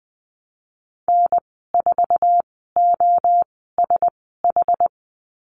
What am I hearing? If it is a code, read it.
Morse code, heard as a keyed tone, about 20 wpm, spaced as for 14 wpm, 700 Hz. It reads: N4OSH
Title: 069 Lesson 12g - Letter d - Call Signs 20wpm